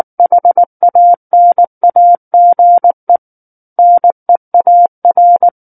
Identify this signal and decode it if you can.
Morse code, heard as a keyed tone, about 19 wpm, 700 Hz. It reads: HANAGE NEAR